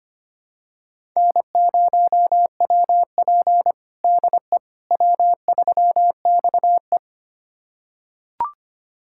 Morse code, heard as a keyed tone, about 25 wpm, 700 Hz. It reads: N0WP DE W3XE
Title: USA Callsign DE Callsign - 4 Characters - 25wpm